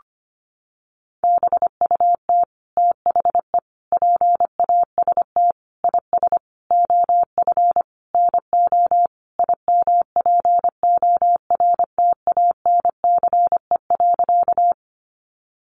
Morse code, heard as a keyed tone, about 25 wpm, 700 Hz. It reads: BUT THE PAST IS OF NO IMPORTANCE.
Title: Sentences from Top 1200 Words 25wpm